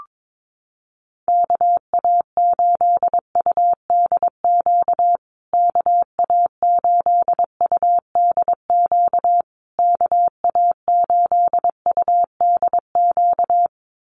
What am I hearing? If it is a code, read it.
Morse code, heard as a keyed tone, about 22 wpm, 700 Hz. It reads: KA8UDQ KA8UDQ KA8UDQ
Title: USA Call Signs Repeated 3x 22wpm